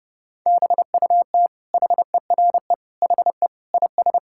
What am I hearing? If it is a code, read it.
Morse code, heard as a keyed tone, about 32 wpm, 700 Hz. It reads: BUT HERE HE IS
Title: Sentences from Top 1600 Words 30wpm